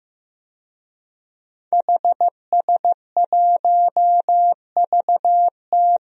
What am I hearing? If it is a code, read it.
Morse code, heard as a keyed tone, about 16 wpm, 700 Hz. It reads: HS1VT
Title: DXCC-100 Call Signs - 5 Characters - Repeated 3x - 15wpm